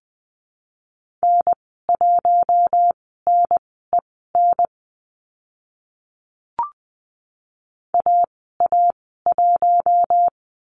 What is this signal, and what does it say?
Morse code, heard as a keyed tone, about 20 wpm, spaced as for 14 wpm, 700 Hz. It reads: N1NEN AA1
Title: 042 Lesson 8g - Number 1 - Call Signs 20wpm